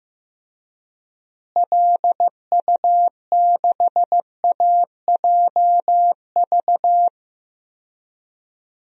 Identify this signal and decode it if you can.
Morse code, heard as a keyed tone, about 15 wpm, 700 Hz. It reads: LU6AJV
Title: DXCC-100 Call Signs - 6 Characters - Repeated 2x - 15wpm